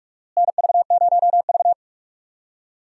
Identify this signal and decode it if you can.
Morse code, heard as a keyed tone, about 44 wpm, 700 Hz. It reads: NV0V
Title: USA Call Signs - 4 Characters - Repeated 2x - 45wpm